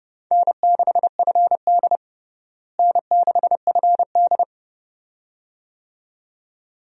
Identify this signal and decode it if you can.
Morse code, heard as a keyed tone, about 30 wpm, 700 Hz. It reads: N6FD N6FD
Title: USA Call Signs - 4 Characters - Repeated 2x - 30wpm